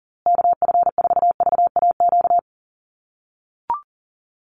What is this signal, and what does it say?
Morse code, heard as a keyed tone, about 40 wpm, 700 Hz. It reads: KF4VAQ
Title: USA Call Signs Repeated 3x 40wpm